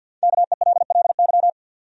Morse code, heard as a keyed tone, about 50 wpm, 700 Hz. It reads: KELLY
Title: USA Common Names - Max 5 Letters - 50wpm